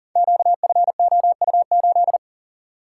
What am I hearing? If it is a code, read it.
Morse code, heard as a keyed tone, about 39 wpm, 700 Hz. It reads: QFOU8
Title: Code Groups - 5 Letter:Numbers - 40wpm